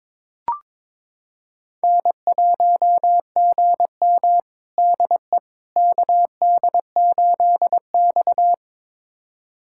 Morse code, heard as a keyed tone, about 22 wpm, 700 Hz. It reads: N1GM DE KD8X
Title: USA Callsign DE Callsign - 4 Characters - 22wpm